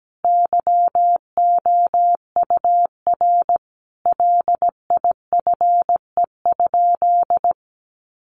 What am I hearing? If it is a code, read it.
Morse code, heard as a keyed tone, about 17 wpm, 700 Hz. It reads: YOUR LIFE?
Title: Sentences from Top 900 Words 17wpm